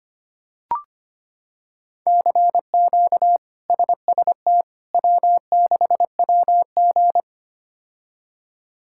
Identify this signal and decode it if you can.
Morse code, heard as a keyed tone, about 25 wpm, 700 Hz. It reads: CQ SST W6WG
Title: K1USN SST QSO Practice - 25wpm